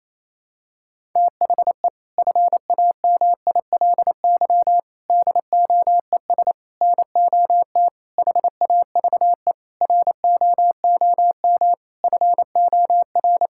Morse code, heard as a keyed tone, about 28 wpm, 700 Hz. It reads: THE FAMILY DOES NOT HAVE ROOM FOR
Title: Sentences from Top 1000 Words - Review Part 5 - 28wpm